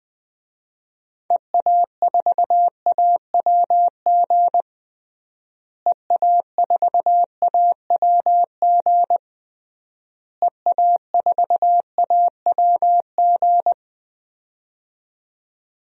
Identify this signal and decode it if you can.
Morse code, heard as a keyed tone, about 20 wpm, 700 Hz. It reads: EA4AWG EA4AWG EA4AWG
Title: DXCC-100 Call Signs - 6 Characters - Repeated 3x - 20wpm